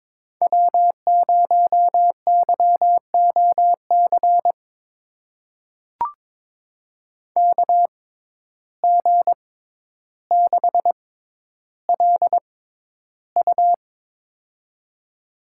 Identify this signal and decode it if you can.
Morse code, heard as a keyed tone, about 22 wpm, spaced as for 11 wpm, 700 Hz. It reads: W0YOC KG6LU
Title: USA Call Signs - 5 Characters - 6x Character Spacing - 22wpm